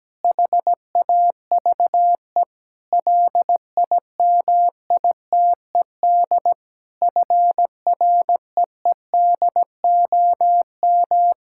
Morse code, heard as a keyed tone, about 17 wpm, 700 Hz. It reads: HAVE LIMITED FREEDOM
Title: Sentences from Top 1000 Words - Review Part 6 - 17wpm